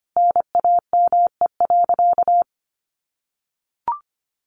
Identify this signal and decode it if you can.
Morse code, heard as a keyed tone, about 25 wpm, 700 Hz. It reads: NAME.
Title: Sentences from Top 1500 Words 25wpm